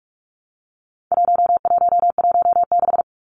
Morse code, heard as a keyed tone, about 44 wpm, 700 Hz. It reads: 1116
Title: Numbers - 4-Digit 45wpm